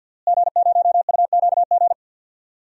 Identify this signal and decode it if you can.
Morse code, heard as a keyed tone, about 51 wpm, 700 Hz. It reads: G0UQG
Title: DXCC-100 Call Signs - 5 Characters - Repeated 2x - 50wpm